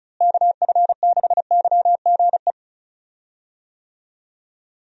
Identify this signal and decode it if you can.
Morse code, heard as a keyed tone, about 34 wpm, 700 Hz. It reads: KF6YGE
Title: USA Call Signs - 6 Characters - Repeated 2x - 35wpm